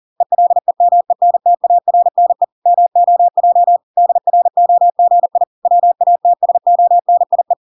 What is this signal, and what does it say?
Morse code, heard as a keyed tone, about 40 wpm, 700 Hz. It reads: ELEMENTARNE MOJ DROGI WATSONIE